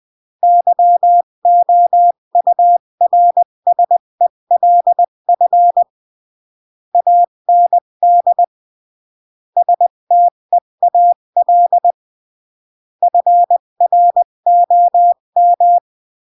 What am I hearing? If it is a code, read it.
Morse code, heard as a keyed tone, about 20 wpm, 700 Hz. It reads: YOURSELF AND STEAL FROM
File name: CW / jodorowsky / 82 commandments 20-15